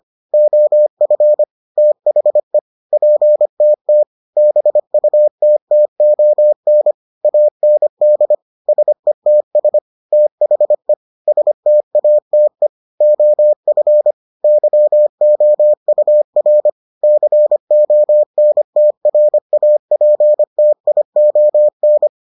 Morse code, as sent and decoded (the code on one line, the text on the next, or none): --- ..-. / - .... / .--. - - / -... ..- - - --- -. / .- -. -.. / ... - ... / - .... / ... - .- - . / --- ..-. / -.-- --- ..- .-. / -.-. --- -. - .-. .- .--. - .. --- -.
OF THE PTT BUTTON AND SETS THE STATE OF YOUR CONTRAPTION